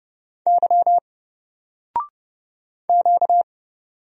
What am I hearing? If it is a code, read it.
Morse code, heard as a keyed tone, about 30 wpm, 700 Hz. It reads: Y Q
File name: Single-Letter-Number-30wpm